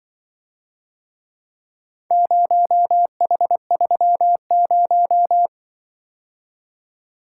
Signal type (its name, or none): Morse code